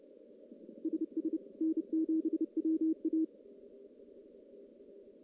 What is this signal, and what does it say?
Morse code, heard as a keyed tone, about 29 wpm, 320 Hz. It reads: SS N7WA
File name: contests / CW-SS-k5tr- -1600